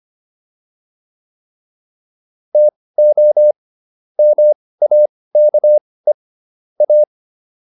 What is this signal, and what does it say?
Morse code, heard as a keyed tone, about 25 wpm, spaced as for 18 wpm, 600 Hz. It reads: TO MAKE A